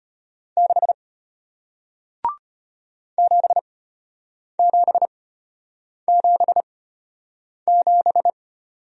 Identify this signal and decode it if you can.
Morse code, heard as a keyed tone, about 34 wpm, spaced as for 8 wpm, 700 Hz. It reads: 6 7777